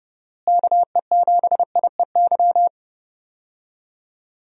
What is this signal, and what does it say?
Morse code, heard as a keyed tone, about 30 wpm, 700 Hz. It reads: KE7IEY
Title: USA Call Signs - 6 Characters - Repeated 3x - 30wpm